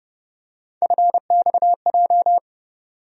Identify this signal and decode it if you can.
Morse code, heard as a keyed tone, about 32 wpm, 700 Hz. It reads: FXJ
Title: USA Call Sign Postfixes - 3 Letters 30wpm